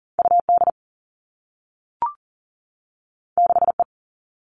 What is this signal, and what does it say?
Morse code, heard as a keyed tone, about 40 wpm, 700 Hz. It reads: UD 6E
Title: Two Challenging Letter-Number Combinations 40wpm